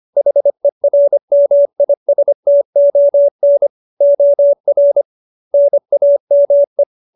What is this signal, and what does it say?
Morse code, heard as a keyed tone, about 25 wpm, 570 Hz. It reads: HERMISTON OR NAME
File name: QSO306 25